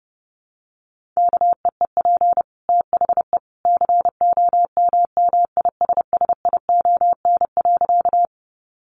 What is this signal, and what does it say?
Morse code, heard as a keyed tone, about 30 wpm, 700 Hz. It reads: KEEP THE COMMISSION.
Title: Sentences from Top 1400 Words 30wpm